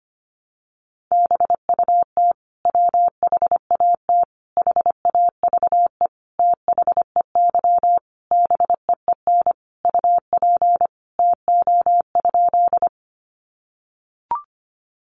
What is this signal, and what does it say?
Morse code, heard as a keyed tone, about 25 wpm, 700 Hz. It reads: BUT WHAT HAVE THEY BEEN UP TO?
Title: Sentences from Top 100 Words 25wpm